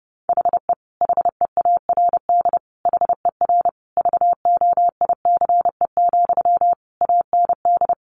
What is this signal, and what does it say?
Morse code, heard as a keyed tone, about 30 wpm, 700 Hz. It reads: HE HEARD HER VOICE, AND
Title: Sentences from Top 1100 Words 30wpm